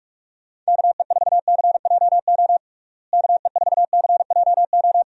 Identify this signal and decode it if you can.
Morse code, heard as a keyed tone, about 46 wpm, 700 Hz. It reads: KE4CJO KE4CJO
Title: USA Call Signs - 6 Characters - Repeated 2x - 45wpm